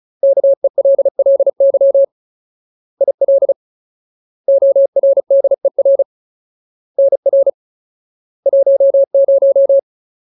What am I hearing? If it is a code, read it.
Morse code, heard as a keyed tone, about 34 wpm, 550 Hz. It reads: KELLY IL ORDER NR 10
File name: Ultimate Word List / 35WPM Ultimate